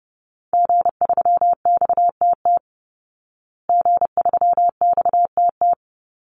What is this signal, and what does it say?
Morse code, heard as a keyed tone, about 30 wpm, 700 Hz. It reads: G3XTT G3XTT
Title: Call Signs Repeated 2x - 30wpm